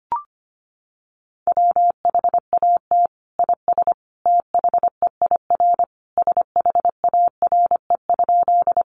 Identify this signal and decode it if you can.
Morse code, heard as a keyed tone, about 25 wpm, 700 Hz. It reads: WHAT IS THEIR SHARE?
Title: Sentences from Top 900 Words 25wpm